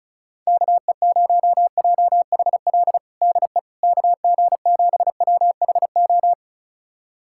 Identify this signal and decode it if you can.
Morse code, heard as a keyed tone, about 36 wpm, 700 Hz. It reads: KE0JHL DE KG7WHO
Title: USA Callsign DE Callsign 35wpm